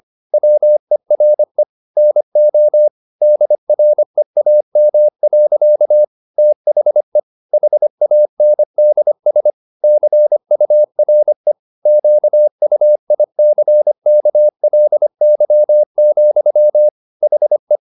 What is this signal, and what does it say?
Morse code, heard as a keyed tone, about 25 wpm, 600 Hz. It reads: WERE NO DREAM. THE HANDS CURE QUICKLY, HE